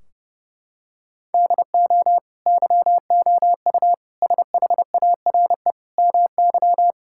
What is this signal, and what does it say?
Morse code, heard as a keyed tone, about 32 wpm, 700 Hz. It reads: DO YOU SHARE MY
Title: Sentences from Top 1600 Words 30wpm